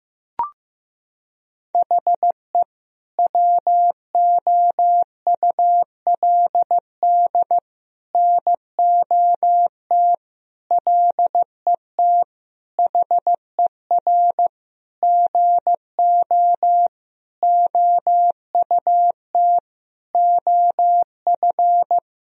Morse code, heard as a keyed tone, about 15 wpm, 700 Hz. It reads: HE WOULD NOT LET HER GO OUT OF
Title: Sentences from Top 1000 Words - Review Part 2 - 15wpm